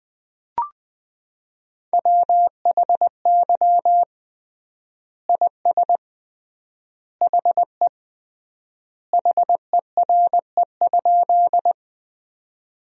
Morse code, heard as a keyed tone, about 21 wpm, 700 Hz. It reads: WHY IS HE HERE?
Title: 3x Word-Spacing - Sentences from Top 200 Words - 20wpm